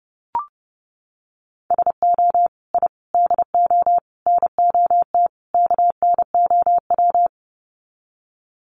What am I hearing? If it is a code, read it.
Morse code, heard as a keyed tone, about 30 wpm, 700 Hz. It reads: SO I DO NOT KNOW
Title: Sets of 5 Words Encore 30wpm